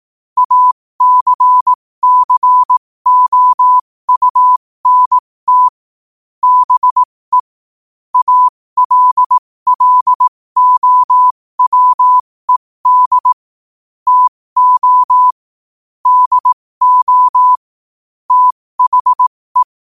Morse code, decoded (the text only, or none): ACCOUNT BE ALLOWED TO DO THE